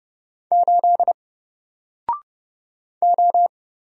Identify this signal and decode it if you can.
Morse code, heard as a keyed tone, about 31 wpm, 700 Hz. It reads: 8 O
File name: Single-Letter-Number-30wpm